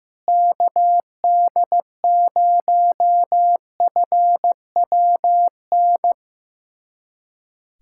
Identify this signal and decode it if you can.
Morse code, heard as a keyed tone, about 15 wpm, 700 Hz. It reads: KD0FWN
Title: USA Callsign DE Callsign - 6 Characters - 15wpm